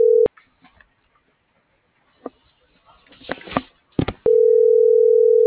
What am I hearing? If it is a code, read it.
Ringback tone (North America), 440 + 480 Hz.